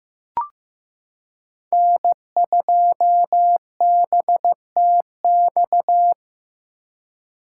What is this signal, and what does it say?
Morse code, heard as a keyed tone, about 15 wpm, 700 Hz. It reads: N2BTX
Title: K1USN SST QSO Practice - 15wpm